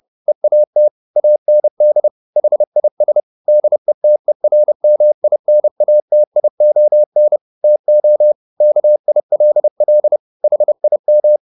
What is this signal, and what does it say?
Morse code, heard as a keyed tone, about 30 wpm, 600 Hz. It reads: EAT AND HIS DETERMINATION TO KILL HIM